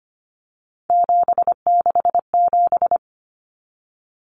Morse code, heard as a keyed tone, about 25 wpm, 700 Hz. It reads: 767
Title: Numbers - 3-Digit 25wpm